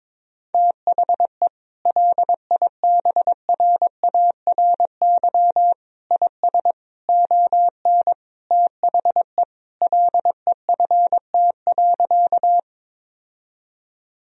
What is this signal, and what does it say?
Morse code, heard as a keyed tone, about 22 wpm, 700 Hz. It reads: THE LIBRARY IS ON THE LEFT.
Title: Sentences from Top 1000 Words - Review Part 4 - 22wpm